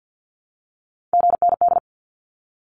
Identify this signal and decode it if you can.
Morse code, heard as a keyed tone, about 50 wpm, 700 Hz. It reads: GND
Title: Abbreviations 50wpm